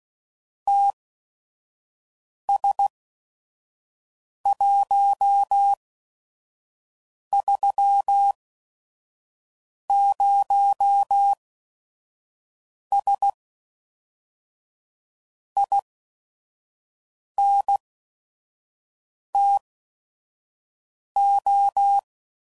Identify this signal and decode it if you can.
Morse code, heard as a keyed tone, about 16 wpm, spaced as for 5 wpm, 800 Hz. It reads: TS130S INTO
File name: CW Code Groups @5WPM / QSOs / WD0FBT 1